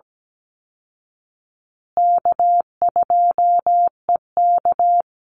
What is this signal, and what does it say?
Morse code, heard as a keyed tone, about 17 wpm, 700 Hz. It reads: K2EK
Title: Easy Call Signs Repeated 2x - 17wpm